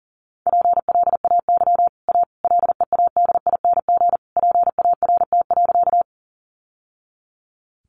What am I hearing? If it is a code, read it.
Morse code, heard as a keyed tone, about 40 wpm, 700 Hz. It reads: PLAY A LEADING PART.
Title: Sentences from Top 1100 Words 40wpm